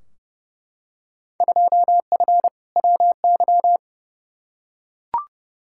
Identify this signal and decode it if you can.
Morse code, heard as a keyed tone, about 30 wpm, 700 Hz. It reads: 2F WY
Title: Field Day Exchanges Encore - 30wpm